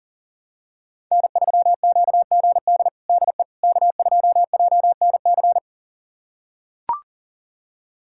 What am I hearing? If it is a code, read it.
Morse code, heard as a keyed tone, about 40 wpm, 700 Hz. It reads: N3QGD DE K2JNC
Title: USA Callsign DE Callsign - 5 Characters - 40wpm